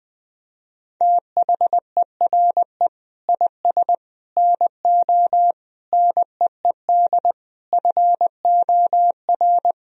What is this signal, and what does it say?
Morse code, heard as a keyed tone, about 21 wpm, 700 Hz. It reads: THERE IS NO NEED FOR
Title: Sentences from Top 1000 Words - Review Part 3 - 20wpm